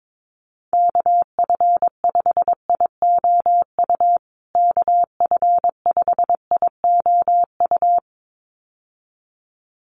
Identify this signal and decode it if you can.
Morse code, heard as a keyed tone, about 22 wpm, 700 Hz. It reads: KF5IOU KF5IOU
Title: USA Call Signs Repeated 2x 22wpm